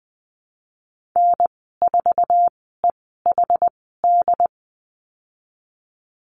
Morse code, heard as a keyed tone, about 20 wpm, spaced as for 15 wpm, 700 Hz. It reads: N4EHD
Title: 076 Lesson 13g - Letter L - Call Signs 20wpm